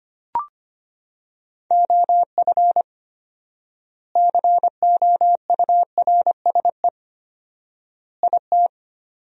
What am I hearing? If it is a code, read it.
Morse code, heard as a keyed tone, about 25 wpm, 700 Hz. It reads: OF COURSE IT
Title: 4x Word-Spacing - Sentences from Top 500 Words - 25wpm